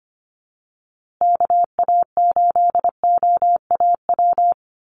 Morse code, heard as a keyed tone, about 25 wpm, 700 Hz. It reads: KA8OAW
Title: USA Call Signs Repeated 3x 25wpm